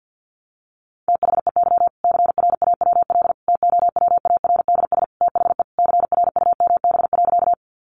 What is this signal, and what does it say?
Morse code, heard as a keyed tone, about 50 wpm, 700 Hz. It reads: THEY CRAWL TOWARDS THE CRUMB.
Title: Easy Phrases 50wpm